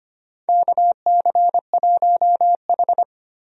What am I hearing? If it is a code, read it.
Morse code, heard as a keyed tone, about 25 wpm, 700 Hz. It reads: KC1H
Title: USA Call Signs - 4 Characters - 25wpm